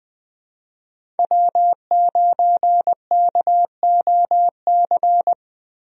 Morse code, heard as a keyed tone, about 20 wpm, 700 Hz. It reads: W9KOC
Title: USA Call Signs - 5 Characters - Repeated 3x - 20wpm